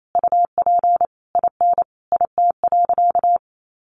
Morse code, heard as a keyed tone, about 28 wpm, 700 Hz. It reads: UP IN IT.
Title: Sentences from Top 1500 Words 28wpm